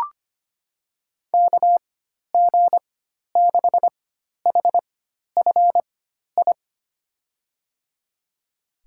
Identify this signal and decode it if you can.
Morse code, heard as a keyed tone, about 25 wpm, spaced as for 12 wpm, 700 Hz. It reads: KG6HFI
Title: USA Call Signs - 4x Character Spacing - 25wpm